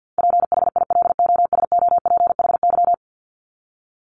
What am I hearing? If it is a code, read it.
Morse code, heard as a keyed tone, about 48 wpm, 700 Hz. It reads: PHILOSOPHY
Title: Top 751-1000 Words 50wpm